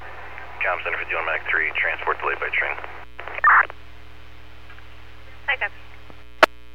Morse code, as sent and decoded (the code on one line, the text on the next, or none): none